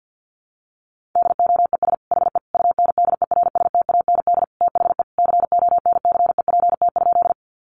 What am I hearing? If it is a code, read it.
Morse code, heard as a keyed tone, about 50 wpm, 700 Hz. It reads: DOES HE UNDERSTAND THE CONCEPT?
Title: Sentences from Top 1100 Words 50wpm